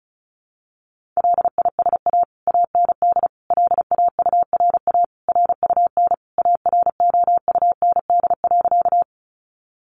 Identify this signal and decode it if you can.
Morse code, heard as a keyed tone, about 34 wpm, 700 Hz. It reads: LISA AND LAURA RUN AROUND.
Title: Additional Easy Phrases 35wpm